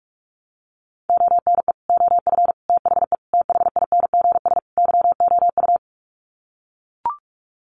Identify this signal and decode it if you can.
Morse code, heard as a keyed tone, about 44 wpm, 700 Hz. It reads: ONE OF THE THINGS YOU
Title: Sets of 5 Words Encore2 45wpm